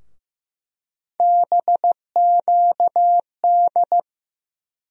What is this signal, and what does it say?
Morse code, heard as a keyed tone, about 15 wpm, 700 Hz. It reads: BQD
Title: USA Call Sign Postfixes - 3 Letters 15wpm